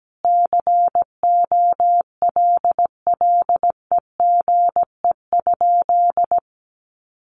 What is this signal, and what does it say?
Morse code, heard as a keyed tone, about 17 wpm, 700 Hz. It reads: COLLEGE?
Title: Sentences from Top 1300 Words 17wpm